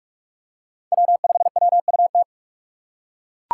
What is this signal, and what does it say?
Morse code, heard as a keyed tone, about 40 wpm, 700 Hz. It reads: W5WUT